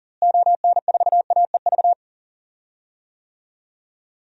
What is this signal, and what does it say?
Morse code, heard as a keyed tone, about 39 wpm, 700 Hz. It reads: ON4AEV
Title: DXCC-100 Call Signs - 6 Characters - Repeated 3x - 40wpm